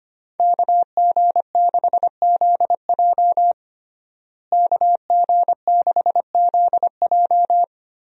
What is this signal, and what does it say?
Morse code, heard as a keyed tone, about 25 wpm, 700 Hz. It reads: KG6ZJ KG6ZJ
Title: USA Call Signs - 5 Characters - Repeated 3x - 25wpm